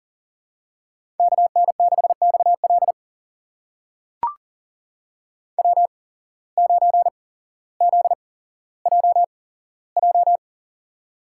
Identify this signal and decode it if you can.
Morse code, heard as a keyed tone, about 41 wpm, spaced as for 17 wpm, 700 Hz. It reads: KN6XL W9ZJJ
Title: USA Call Signs - 5 Characters - 8x Character Spacing - 40wpm